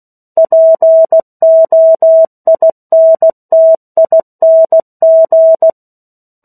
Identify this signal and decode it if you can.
Morse code, heard as a keyed tone, about 16 wpm, 650 Hz. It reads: POINTING
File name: 03-Chapter-03 16 WPM